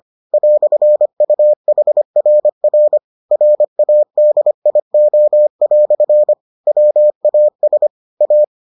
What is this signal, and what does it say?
Morse code, heard as a keyed tone, about 25 wpm, 600 Hz. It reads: "UHRR RADIO" WAS A